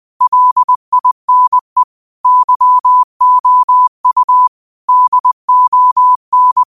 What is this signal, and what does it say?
Morse code, heard as a keyed tone, about 20 wpm, 1000 Hz. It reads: LINE YOU DON